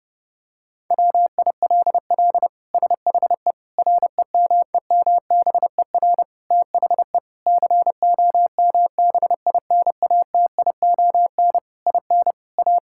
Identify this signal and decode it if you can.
Morse code, heard as a keyed tone, about 29 wpm, 700 Hz. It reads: WILL SHE REMEMBER THE COMBINATION IN A
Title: Sentences from Top 1000 Words - Review Part 6 - 30wpm